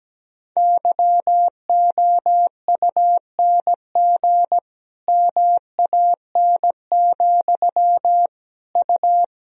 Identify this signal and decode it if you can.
Morse code, heard as a keyed tone, about 17 wpm, 700 Hz. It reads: YOUNG MAN, U
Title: Sentences from Top 400 Words and Abbreviations 17wpm